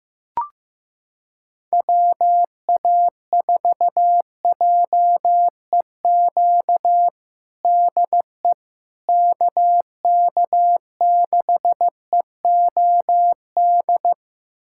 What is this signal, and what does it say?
Morse code, heard as a keyed tone, about 15 wpm, 700 Hz. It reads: WA4JEQ DE KK6EOD
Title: USA Callsign DE Callsign 15wpm